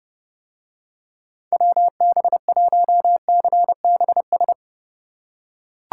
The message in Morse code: WB1CBS

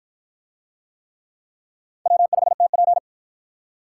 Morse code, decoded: W5TP

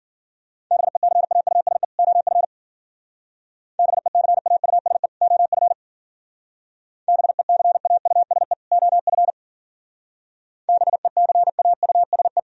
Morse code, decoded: BECAUSE OF BECAUSE OF BECAUSE OF BECAUSE

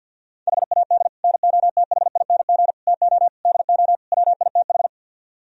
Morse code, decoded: HAD NOTHING TO DO WITH

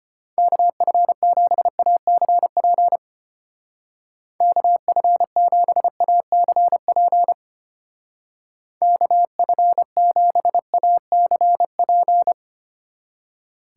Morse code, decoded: KF7ACP KF7ACP KF7ACP

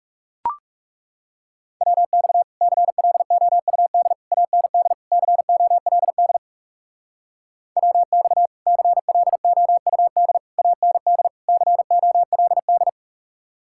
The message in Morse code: WX CLOUD AND COLD WX CLOUD AND COLD

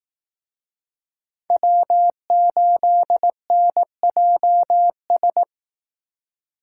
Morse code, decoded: W8NJS

